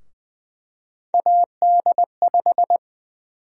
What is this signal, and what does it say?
Morse code, heard as a keyed tone, about 20 wpm, 700 Hz. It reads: AD5
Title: USA Call Sign Prefixes - Repeated 2x - 20wpm